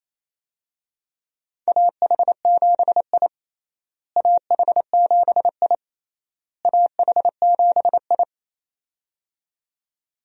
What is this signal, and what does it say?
Morse code, heard as a keyed tone, about 28 wpm, 700 Hz. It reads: AH7I AH7I AH7I
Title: USA Call Signs - 4 Characters - Repeated 3x - 28wpm